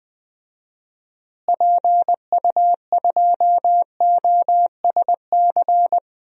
Morse code, heard as a keyed tone, about 20 wpm, 700 Hz. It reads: PU2OSC
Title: DXCC-100 Call Signs - 6 Characters - Repeated 2x - 20wpm